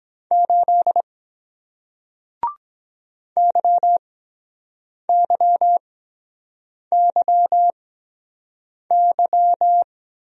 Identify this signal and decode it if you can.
Morse code, heard as a keyed tone, about 23 wpm, spaced as for 7 wpm, 700 Hz. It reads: 8 YYYY